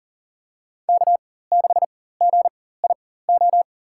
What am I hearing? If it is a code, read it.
Morse code, heard as a keyed tone, about 37 wpm, spaced as for 18 wpm, 700 Hz. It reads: K6GIO
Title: USA Call Signs - 5 Characters - 4x Character Spacing - 40wpm